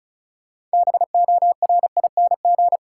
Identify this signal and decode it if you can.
Morse code, heard as a keyed tone, about 36 wpm, 700 Hz. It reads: BORING
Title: Six Letter Words - 35wpm